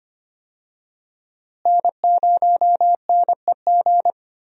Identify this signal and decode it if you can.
Morse code, heard as a keyed tone, about 25 wpm, 700 Hz. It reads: N0NEG
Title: USA Call Signs - 5 Characters - Repeated 3x - 25wpm